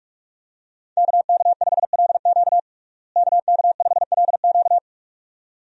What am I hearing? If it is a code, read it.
Morse code, heard as a keyed tone, about 47 wpm, 700 Hz. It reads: KK5LQ KK5LQ
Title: USA Call Signs - 5 Characters - Repeated 2x - 45wpm